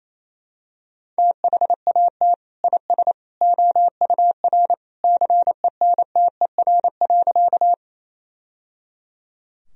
Morse code, heard as a keyed tone, about 28 wpm, 700 Hz. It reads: THAT IS OUR CENTER.
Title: Sentences from Top 1000 Words - Review Part 2 - 28wpm